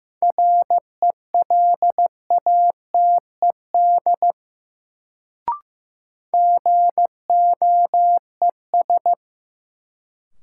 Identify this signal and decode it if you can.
Morse code, heard as a keyed tone, about 15 wpm, 700 Hz. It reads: RELATED GOES
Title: Sentences from Top 1000 Words - Review Part 5 - New Word Drill - 15wpm